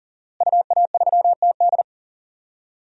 Morse code, heard as a keyed tone, about 40 wpm, 700 Hz. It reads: UA3TD
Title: DXCC-100 Call Signs - 5 Characters - Repeated 3x - 40wpm